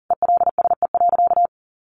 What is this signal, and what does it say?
Morse code, heard as a keyed tone, about 40 wpm, 700 Hz. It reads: ELSE.
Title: Sentences from Top 1400 Words 40wpm